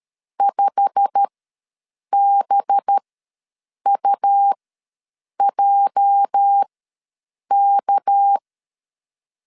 Morse code, heard as a keyed tone, about 13 wpm, spaced as for 7 wpm, 800 Hz. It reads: 5BUJK
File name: LEZIONI / 8 YJ45 / 8 YJ45 ws 12wpm 3146